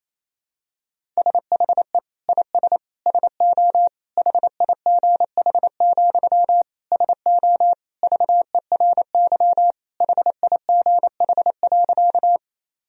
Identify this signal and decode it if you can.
Morse code, heard as a keyed tone, about 28 wpm, 700 Hz. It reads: SHE IS SO HIGH, SO VERY HIGH.
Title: Sentences from Top 1000 Words - Review Part 1 - 28wpm